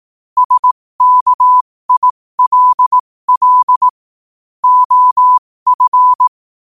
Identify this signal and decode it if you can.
Morse code, heard as a keyed tone, about 18 wpm, spaced as for 15 wpm, 1000 Hz. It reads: SKILL OF
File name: qotdfiles / 15wpm